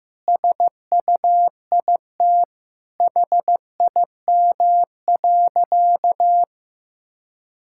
Morse code, heard as a keyed tone, about 15 wpm, 700 Hz. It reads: SUIT HIM.
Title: Sentences from Top 1600 Words 15wpm